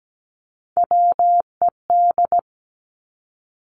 Morse code, heard as a keyed tone, about 17 wpm, 700 Hz. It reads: WED
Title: Three Challenging Letter-Number Combinations 17wpm